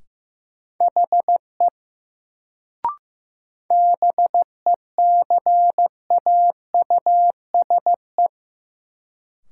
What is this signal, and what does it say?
Morse code, heard as a keyed tone, about 15 wpm, 700 Hz. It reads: HE BECAUSE
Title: Sets of 2 Words - New Word Drill - 15wpm